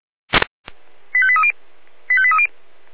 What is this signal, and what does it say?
Five-tone selective call (ZVEI), heard as quick runs of five tones, tones of about 70 ms. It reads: 85830 85830